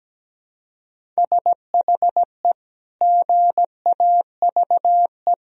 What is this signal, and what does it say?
Morse code, heard as a keyed tone, about 17 wpm, 700 Hz. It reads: SHE GAVE
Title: Sentences from Top 1000 Words - Review Part 1 - 17wpm